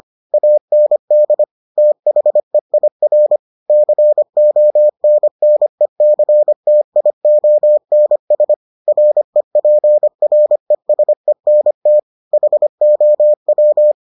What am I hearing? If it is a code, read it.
Morse code, heard as a keyed tone, about 25 wpm, 600 Hz. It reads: AND THEIR CONNECTIONS REPRESENT HOW